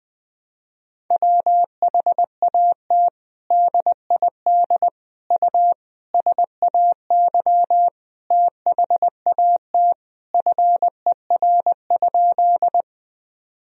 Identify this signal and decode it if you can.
Morse code, heard as a keyed tone, about 20 wpm, 700 Hz. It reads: WHAT DID U SAY THAT FER?
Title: Sentences from Top 200 Words and Abbreviations 20wpm